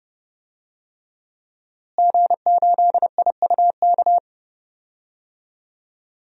Morse code, heard as a keyed tone, about 30 wpm, 700 Hz. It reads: G8IUK
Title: DXCC-100 Call Signs - 5 Characters - Repeated 3x - 30wpm